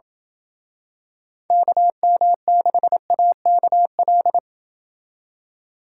Morse code, heard as a keyed tone, about 28 wpm, 700 Hz. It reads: KM6AKL